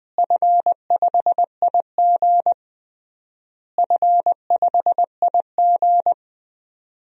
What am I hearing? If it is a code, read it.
Morse code, heard as a keyed tone, about 20 wpm, 700 Hz. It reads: F5IG F5IG
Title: DXCC-100 Call Signs - 4 Characters - Repeated 3x - 20wpm